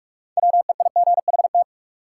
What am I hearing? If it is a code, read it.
Morse code, heard as a keyed tone, about 46 wpm, 700 Hz. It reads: WEIGHT
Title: Sentences from Top 800 Words - New Word Drill - 45wpm